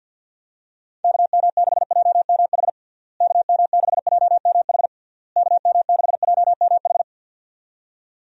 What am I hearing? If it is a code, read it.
Morse code, heard as a keyed tone, about 50 wpm, 700 Hz. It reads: KM6JMH KM6JMH KM6JMH
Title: USA Call Signs - 6 Characters - Repeated 3x - 50wpm